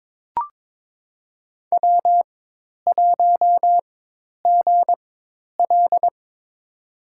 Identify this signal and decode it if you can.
Morse code, heard as a keyed tone, about 22 wpm, spaced as for 10 wpm, 700 Hz. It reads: W1GL
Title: USA Call Signs - 4 Characters - 4x Character Spacing - 22wpm